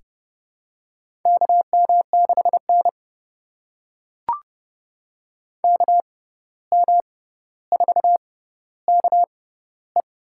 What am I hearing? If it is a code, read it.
Morse code, heard as a keyed tone, about 30 wpm, spaced as for 14 wpm, 700 Hz. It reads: KM6N KM4KE
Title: USA Call Signs - 6x Character Spacing - 30wpm